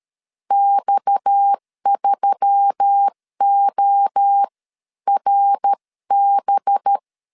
Morse code, heard as a keyed tone, about 13 wpm, spaced as for 11 wpm, 800 Hz. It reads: X3ORB
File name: CW / DET / 1 6 1500C S C 12wpm 3533